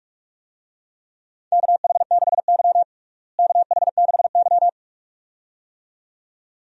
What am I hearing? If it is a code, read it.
Morse code, heard as a keyed tone, about 42 wpm, 700 Hz. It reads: KH6Y KH6Y
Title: DXCC-100 Call Signs - 4 Characters - Repeated 2x - 45wpm